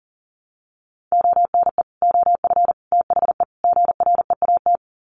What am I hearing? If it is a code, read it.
Morse code, heard as a keyed tone, about 40 wpm, 700 Hz. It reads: ONE OF THE GREAT
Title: Sets of 4 Words 40wpm